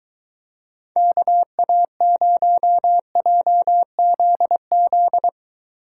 Morse code, heard as a keyed tone, about 22 wpm, 700 Hz. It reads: KA0JZZ